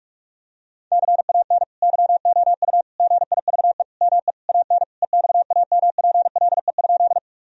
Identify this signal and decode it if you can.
Morse code, heard as a keyed tone, about 46 wpm, 700 Hz. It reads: CAN YOU GIVE ME AN EXAMPLE?
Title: Sentences from Top 1000 Words - Review Part 1 - 45wpm